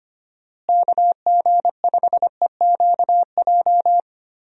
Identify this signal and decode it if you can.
Morse code, heard as a keyed tone, about 25 wpm, 700 Hz. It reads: KG5EQJ
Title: USA Call Signs - 6 Characters - Repeated 3x - 25wpm